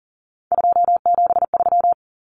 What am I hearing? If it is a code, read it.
Morse code, heard as a keyed tone, about 40 wpm, 700 Hz. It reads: 273
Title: Common Cut Numbers - 3-Digit 40wpm